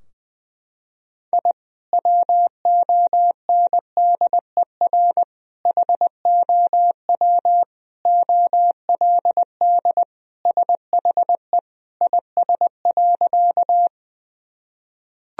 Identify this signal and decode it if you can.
Morse code, heard as a keyed tone, about 20 wpm, 700 Hz. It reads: I WONDER HOW OLD SHE IS.
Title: Sentences from Top 1000 Words - Review Part 3 - 20wpm